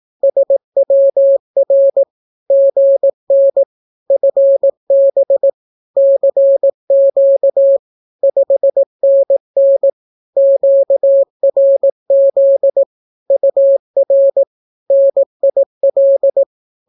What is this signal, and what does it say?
Morse code, heard as a keyed tone, about 18 wpm, 550 Hz. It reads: SWR GN FB CQ 5NN QRZ UR NIL
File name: naukapodstaw / skroty / skroty